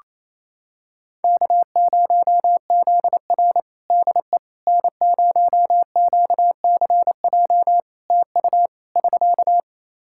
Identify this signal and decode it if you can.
Morse code, heard as a keyed tone, about 28 wpm, 700 Hz. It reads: K0ZR DE N0QCJ TU <SK>
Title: Practice QSO - Exchange 4 - Ending - 28wpm